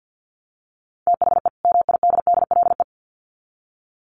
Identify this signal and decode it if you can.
Morse code, heard as a keyed tone, about 50 wpm, 700 Hz. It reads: THE MIDDLE
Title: Sets of 2 Words 50wpm